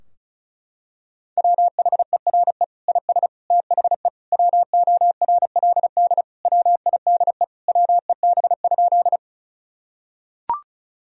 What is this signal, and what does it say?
Morse code, heard as a keyed tone, about 34 wpm, 700 Hz. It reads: WHERE IS THE WORLD WIDE WEB?
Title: Sentences from Top 1000 Words - Review Part 4 - 35wpm